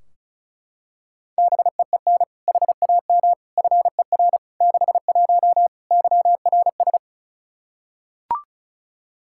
Morse code, heard as a keyed tone, about 36 wpm, 700 Hz. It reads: BEEN HAM FER 61 YRS
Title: K1USN SST QSO Practice 35wpm